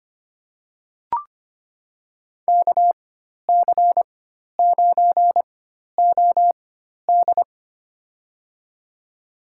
Morse code, heard as a keyed tone, about 25 wpm, spaced as for 12 wpm, 700 Hz. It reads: KC9OD
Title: USA Call Signs - 5 Characters - 4x Character Spacing - 25wpm